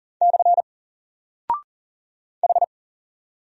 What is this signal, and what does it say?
Morse code, heard as a keyed tone, about 40 wpm, 700 Hz. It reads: / H
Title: Single Character (Letter:Number:Punctuation:Prosign) 40wpm